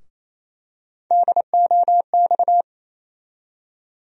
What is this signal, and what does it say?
Morse code, heard as a keyed tone, about 28 wpm, 700 Hz. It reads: DOX
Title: USA Call Sign Postfixes - 3 Letters 28wpm